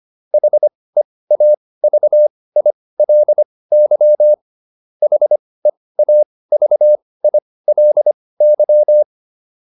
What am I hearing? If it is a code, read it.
Morse code, heard as a keyed tone, about 25 wpm, spaced as for 18 wpm, 600 Hz. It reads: HEAVILY HEAVILY